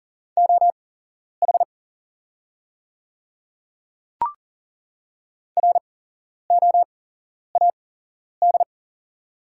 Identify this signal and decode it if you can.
Morse code, heard as a keyed tone, about 40 wpm, spaced as for 11 wpm, 700 Hz. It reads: OH ROAD
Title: 8x Character Spacing - Top 500 Words - 40wpm